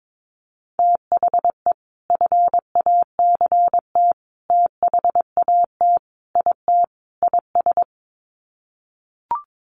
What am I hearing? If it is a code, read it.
Morse code, heard as a keyed tone, about 22 wpm, 700 Hz. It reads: THE FACT THAT IT IS
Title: Sets of 5 Words Encore2 22wpm